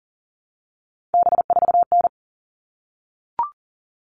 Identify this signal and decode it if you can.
Morse code, heard as a keyed tone, about 38 wpm, 700 Hz. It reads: B4N
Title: All Cut Numbers - 3-Digit 40wpm